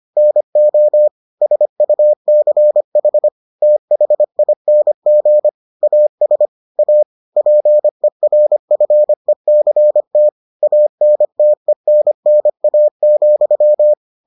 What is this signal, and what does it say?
Morse code, heard as a keyed tone, about 25 wpm, 600 Hz. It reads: NO SUCH THING AS A PERFECT ANTENNA,